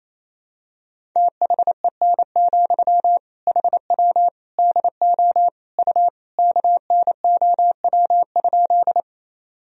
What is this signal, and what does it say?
Morse code, heard as a keyed tone, about 28 wpm, 700 Hz. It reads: THEN, HW DO U KNOW?
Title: Sentences from Top 100 Words and Abbreviations 28wpm